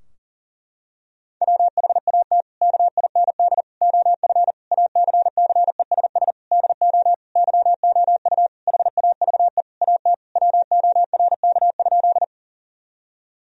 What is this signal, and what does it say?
Morse code, heard as a keyed tone, about 40 wpm, 700 Hz. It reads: WHAT KIND OF ACCESS DO YOU HAVE AT WORK?
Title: Sentences from Top 1000 Words - Review Part 4 - 40wpm